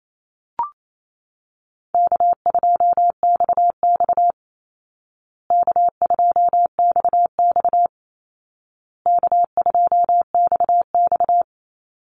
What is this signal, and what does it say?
Morse code, heard as a keyed tone, about 28 wpm, 700 Hz. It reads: K2XX K2XX K2XX
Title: Call Signs Repeated 2x - 28wpm